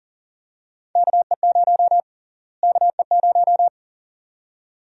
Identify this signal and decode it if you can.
Morse code, heard as a keyed tone, about 40 wpm, 700 Hz. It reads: KE0 KE0
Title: USA Call Sign Prefixes - Repeated 2x - 40wpm